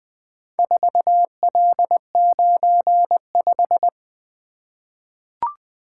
Morse code, heard as a keyed tone, about 20 wpm, 700 Hz. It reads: 4L95
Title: Four Letter-Number Combinations - 20wpm